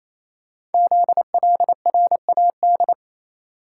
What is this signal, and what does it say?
Morse code, heard as a keyed tone, about 29 wpm, 700 Hz. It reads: ZLRAD
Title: Code Groups - 5 Letter:Numbers - 28wpm